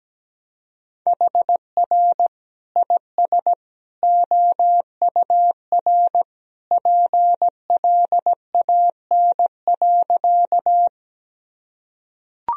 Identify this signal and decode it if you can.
Morse code, heard as a keyed tone, about 17 wpm, 700 Hz. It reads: HR IS OUR PLAN.
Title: Sentences from Top 500 Words and Abbreviations 17wpm